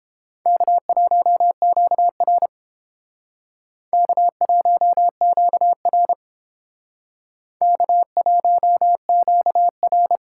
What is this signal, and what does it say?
Morse code, heard as a keyed tone, about 30 wpm, 700 Hz. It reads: K1QR K1QR K1QR